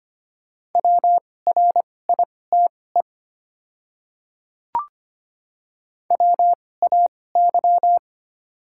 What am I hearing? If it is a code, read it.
Morse code, heard as a keyed tone, about 25 wpm, spaced as for 18 wpm, 690 Hz. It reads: WRITE WAY
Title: 2x Character Spacing - Top 100 Words - 25wpm